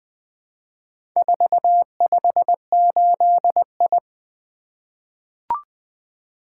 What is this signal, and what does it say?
Morse code, heard as a keyed tone, about 20 wpm, 700 Hz. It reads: 458I
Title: Four Letter-Number Combinations - 20wpm